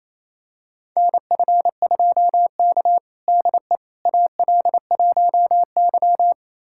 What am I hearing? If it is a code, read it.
Morse code, heard as a keyed tone, about 28 wpm, 700 Hz. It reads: NF2K DE AL1Y
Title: USA Callsign DE Callsign - 4 Characters - 28wpm